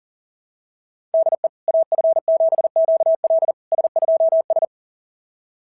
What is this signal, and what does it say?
Morse code, heard as a keyed tone, about 37 wpm, 690 Hz. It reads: DE AF7QL S2S
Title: SOTA QSO Practice 40wpm